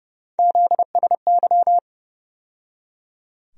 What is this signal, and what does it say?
Morse code, heard as a keyed tone, about 30 wpm, 700 Hz. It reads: ZSY
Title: USA Call Sign Postfixes - 3 Letters 30wpm